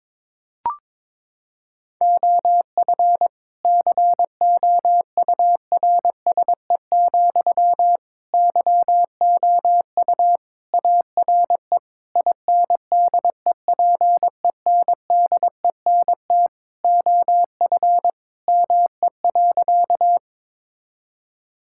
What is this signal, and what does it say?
Morse code, heard as a keyed tone, about 22 wpm, 700 Hz. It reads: OF COURSE, YOU ARE INDEPENDENT OF ME.
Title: Sentences from Top 1000 Words - Review Part 6 - 22wpm